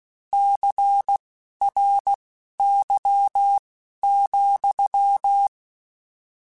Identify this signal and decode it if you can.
Morse code, heard as a keyed tone, about 16 wpm, spaced as for 11 wpm, 800 Hz. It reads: CRY,
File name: Misc / Misc-mp3s / Word Pairs @10